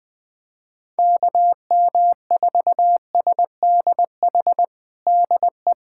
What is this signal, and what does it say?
Morse code, heard as a keyed tone, about 20 wpm, 700 Hz. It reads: KM4SDH DE
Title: USA Callsign DE Callsign 20wpm